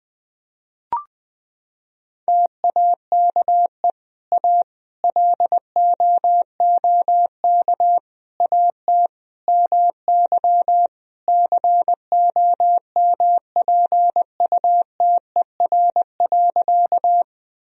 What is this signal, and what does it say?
Morse code, heard as a keyed tone, about 20 wpm, 700 Hz. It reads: TAKE A LOOK AT MY COMPUTER.
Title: Sentences from Top 1000 Words - Review Part 3 - 20wpm